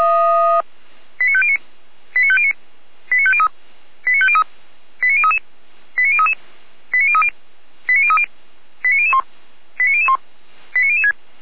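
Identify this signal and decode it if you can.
Five-tone selective call (ZVEI), heard as quick runs of five tones, tones of about 70 ms. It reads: 89509 89509 89503 89503 89033 89033 89030 89030 89001 89001 89006
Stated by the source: ZVEI five-tone selective call 89011 received on SDR.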